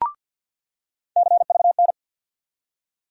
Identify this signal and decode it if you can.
Morse code, heard as a keyed tone, about 48 wpm, 700 Hz. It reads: CVN
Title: USA Call Sign Postfixes - 3 Letters 50wpm